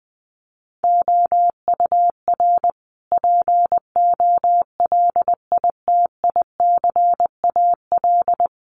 Morse code, heard as a keyed tone, about 20 wpm, 700 Hz. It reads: OUR POLITICAL
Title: Sentences from Top 900 Words 20wpm